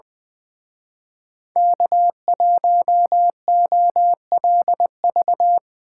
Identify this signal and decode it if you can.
Morse code, heard as a keyed tone, about 22 wpm, 700 Hz. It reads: K1OLV
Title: Speed Racing - USA 1x3 Call Signs 20wpm